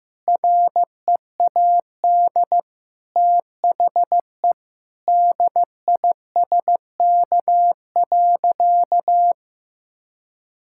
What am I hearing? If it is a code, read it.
Morse code, heard as a keyed tone, about 15 wpm, 700 Hz. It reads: READ THE DISK.
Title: Sentences from Top 1000 Words - Review Part 6 - 15wpm